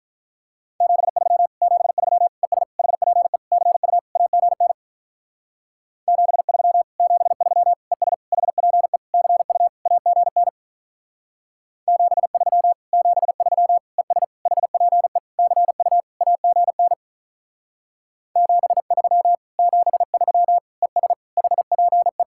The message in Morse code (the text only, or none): 73 73 ES HPE CU AGN 73 73 ES HPE CU AGN 73 73 ES HPE CU AGN 73 73 ES HPE